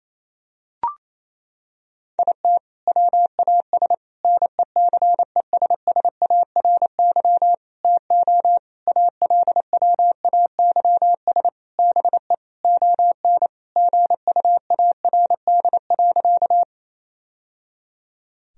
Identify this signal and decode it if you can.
Morse code, heard as a keyed tone, about 28 wpm, 700 Hz. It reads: IT WAS NECESSARY TO ALWAYS BE ON GUARD.
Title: Sentences from Top 1000 Words - Review Part 4 - 28wpm